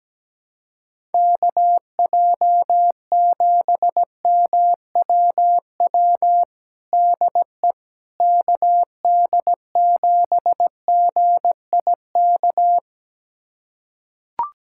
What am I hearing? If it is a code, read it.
Morse code, heard as a keyed tone, about 17 wpm, 700 Hz. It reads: KJ7MWW DE KD7GIK
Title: USA Callsign DE Callsign - 6 Characters - 17wpm